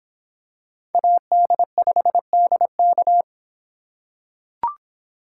Morse code, heard as a keyed tone, about 26 wpm, 700 Hz. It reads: AD5DK